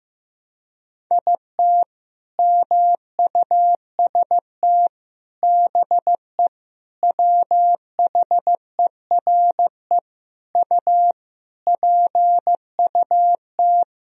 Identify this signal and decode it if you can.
Morse code, heard as a keyed tone, about 15 wpm, 700 Hz. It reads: IT MUST BE WHERE U PUT
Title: Sentences from Top 200 Words and Abbreviations 15wpm